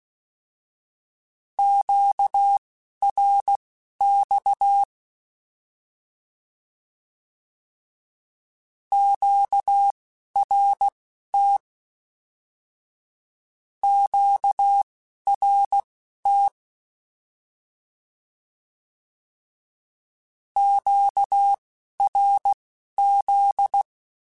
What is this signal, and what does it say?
Morse code, heard as a keyed tone, about 16 wpm, spaced as for 11 wpm, 800 Hz. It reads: QRX QRT QRT QRZ
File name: Q Calls / Q Calls-mp3s / Q Calls Twice @10